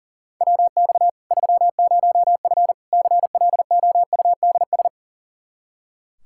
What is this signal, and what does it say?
Morse code, heard as a keyed tone, about 37 wpm, 700 Hz. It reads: WX 30F CLOUDS
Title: Practice QSO - Exchange 2 - Info - 40wpm